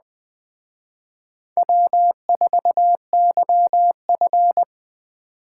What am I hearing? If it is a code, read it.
Morse code, heard as a keyed tone, about 20 wpm, 700 Hz. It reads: W4YF
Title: Speed Racing - USA 1x2 Call Signs 17wpm